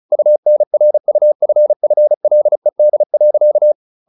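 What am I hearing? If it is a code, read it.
Morse code, heard as a keyed tone, about 34 wpm, 600 Hz. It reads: UNRUFFLED.
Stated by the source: Morse practice at 35 wpm from CW med Gnister, sending Ken Follett's novel Never in English.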